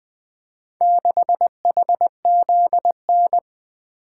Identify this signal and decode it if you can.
Morse code, heard as a keyed tone, about 20 wpm, 700 Hz. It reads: 6HZN
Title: Four Letter-Number Combinations - 20wpm